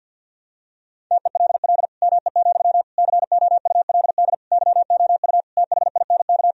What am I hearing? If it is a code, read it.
Morse code, heard as a keyed tone, about 52 wpm, 700 Hz. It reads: TELL ME, COULD YOU THINK
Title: Sentences from Top 1000 Words - Review Part 2 - 50wpm